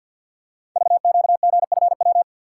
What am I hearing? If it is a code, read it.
Morse code, heard as a keyed tone, about 51 wpm, 700 Hz. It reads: VQGFW
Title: Code Groups - 5 Letters - 50wpm